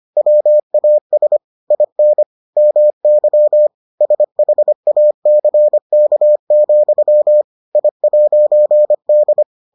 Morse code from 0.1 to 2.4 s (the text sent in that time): WAS IN